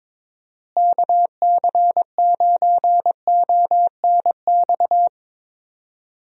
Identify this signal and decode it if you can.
Morse code, heard as a keyed tone, about 21 wpm, 700 Hz. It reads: KC9ONX
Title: USA Call Signs - 6 Characters - Repeated 3x - 22wpm